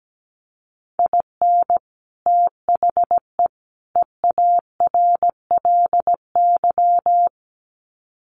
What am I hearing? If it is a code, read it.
Morse code, heard as a keyed tone, about 17 wpm, 700 Hz. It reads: IN THE EARLY
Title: Sets of 3 Words 17wpm